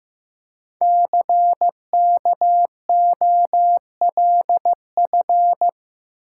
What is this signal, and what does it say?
Morse code, heard as a keyed tone, about 15 wpm, 700 Hz. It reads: CKOLF
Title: Code Groups - 5 Letters - 15wpm